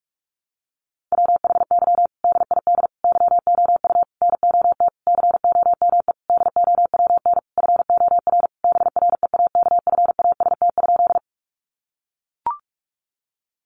Morse code, heard as a keyed tone, about 44 wpm, 700 Hz. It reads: WHY DID YOU NOT COME DOWN FOR BREAKFAST?
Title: Sentences from Top 1500 Words 45wpm